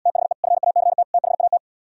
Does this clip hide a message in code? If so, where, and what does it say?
Yes: Morse code from 0.1 to 1.6 s, heard as a keyed tone, about 77 wpm, 700 Hz. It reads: THE FACT THAT